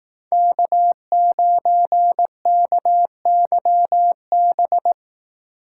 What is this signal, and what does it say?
Morse code, heard as a keyed tone, about 18 wpm, 700 Hz. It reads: K9KYB
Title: Speed Racing - USA 1x3 Call Signs 15wpm